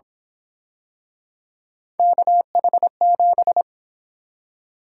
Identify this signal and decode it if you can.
Morse code, heard as a keyed tone, about 27 wpm, 700 Hz. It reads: KH7
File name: Speed-Racing - Three Letter-Number Combinations - 22wpm